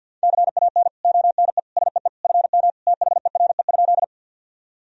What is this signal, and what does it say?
Morse code, heard as a keyed tone, about 52 wpm, 700 Hz. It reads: CAN ONE SEE FM THERE?
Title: Sentences from Top 100 Words and Abbreviations 50wpm